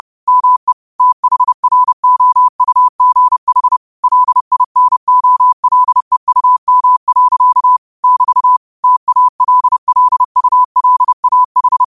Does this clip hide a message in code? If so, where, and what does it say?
Yes: Morse code from 0.3 to 11.8 s, heard as a keyed tone, about 30 wpm, 1000 Hz. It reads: ME THROUGH LINOLEUM. <BT> TALLULAH